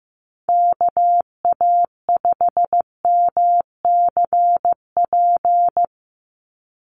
Morse code, heard as a keyed tone, about 15 wpm, 700 Hz. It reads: KA5MCP
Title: USA Call Signs 15wpm